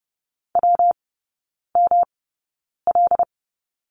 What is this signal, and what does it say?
Morse code, heard as a keyed tone, about 30 wpm, spaced as for 9 wpm, 700 Hz. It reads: WML
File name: Lesson 19e - Letter w - Three Characters 30wpm